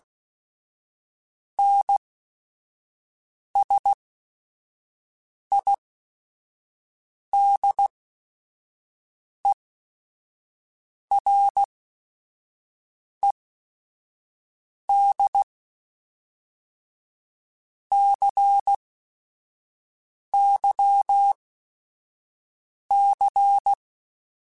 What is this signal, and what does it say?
Morse code, heard as a keyed tone, about 16 wpm, spaced as for 5 wpm, 800 Hz. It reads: NSIDERED CYC
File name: Technical Subjects / Technical Subjects-mp3s / XL @5